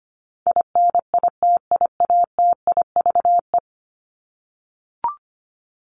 Morse code, heard as a keyed tone, about 25 wpm, 700 Hz. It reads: INITIATIVE
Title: Top 1751-2000 Words 25wpm